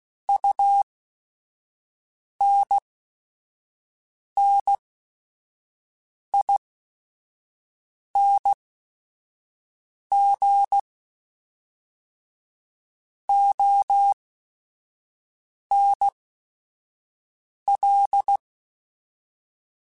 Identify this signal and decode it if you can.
Morse code, heard as a keyed tone, about 16 wpm, spaced as for 5 wpm, 800 Hz. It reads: UNNING ONL